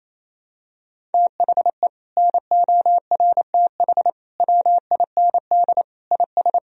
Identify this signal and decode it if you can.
Morse code, heard as a keyed tone, about 27 wpm, 700 Hz. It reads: THE NORTH WIND IS
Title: Sentences from Top 1000 Words - Review Part 2 - 28wpm